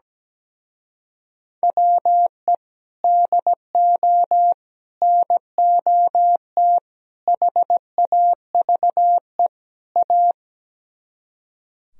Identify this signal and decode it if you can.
Morse code, heard as a keyed tone, about 17 wpm, 700 Hz. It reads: WE DO NOT HAVE A
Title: Speed-Racing - Sets of 5 Words 17wpm